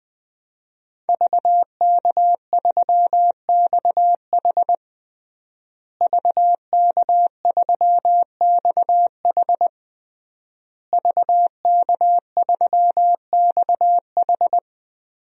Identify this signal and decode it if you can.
Morse code, heard as a keyed tone, about 20 wpm, 700 Hz. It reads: VK3XH VK3XH VK3XH
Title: DXCC-100 Call Signs - 5 Characters - Repeated 3x - 20wpm